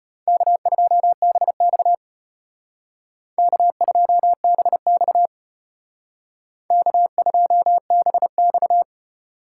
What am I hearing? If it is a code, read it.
Morse code, heard as a keyed tone, about 32 wpm, 700 Hz. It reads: K2BX K2BX K2BX